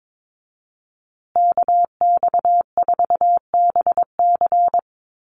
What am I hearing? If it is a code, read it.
Morse code, heard as a keyed tone, about 22 wpm, 700 Hz. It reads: KX4BC
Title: USA Call Signs Repeated 2x 22wpm